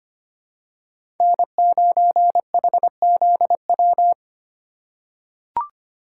Morse code, heard as a keyed tone, about 25 wpm, 700 Hz. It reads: N9HZW
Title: USA Call Signs - 5 Characters - 8x Character Spacing - 25wpm